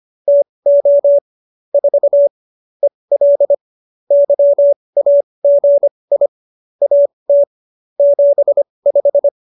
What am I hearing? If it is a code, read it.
Morse code, heard as a keyed tone, about 25 wpm, spaced as for 20 wpm, 570 Hz. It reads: TO 4 EL YAGI AT 75
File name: QSO302 20